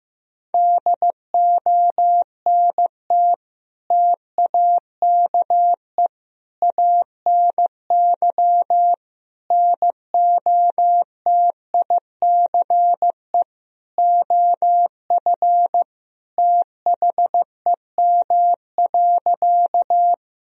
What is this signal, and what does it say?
Morse code, heard as a keyed tone, about 15 wpm, 700 Hz. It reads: DONT TAKE ANY NOTICE OF THEM.
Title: Sentences from Top 1000 Words - Review Part 2 - 15wpm